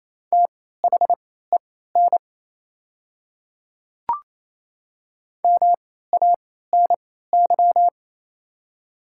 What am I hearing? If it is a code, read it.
Morse code, heard as a keyed tone, about 28 wpm, spaced as for 16 wpm, 700 Hz. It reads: THEN MANY
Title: 3x Character Spacing - Top 100 Words - 28wpm